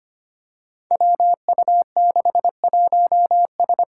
Morse code, heard as a keyed tone, about 25 wpm, 700 Hz. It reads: WU61S
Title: Five Challenging Letter-Number Combinations 25wpm